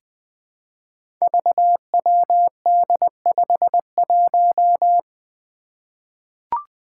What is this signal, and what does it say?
Morse code, heard as a keyed tone, about 21 wpm, 700 Hz. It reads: VWD51
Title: Five Challenging Letter-Number Combinations 20wpm